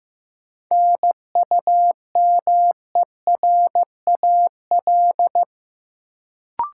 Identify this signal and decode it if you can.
Morse code, heard as a keyed tone, about 15 wpm, 700 Hz. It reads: NUMERAL
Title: Sentences from Top 1000 Words - Review Part 2 - New Word Drill - 15wpm